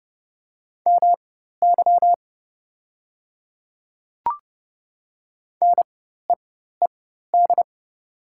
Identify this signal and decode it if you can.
Morse code, heard as a keyed tone, about 30 wpm, spaced as for 14 wpm, 690 Hz. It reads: MY NEED
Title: 4x Character Spacing - Top 200 Words - 30wpm